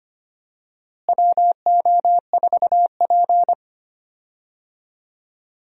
Morse code, heard as a keyed tone, about 25 wpm, 700 Hz. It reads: WO4P
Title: USA Call Signs - 4 Characters - Repeated 3x - 25wpm